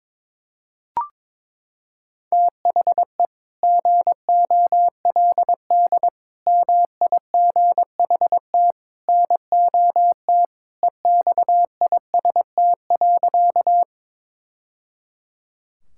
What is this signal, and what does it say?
Morse code, heard as a keyed tone, about 22 wpm, 700 Hz. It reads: THE GOLD MIGHT NOT EXIST.
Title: Sentences from Top 1000 Words - Review Part 5 - 22wpm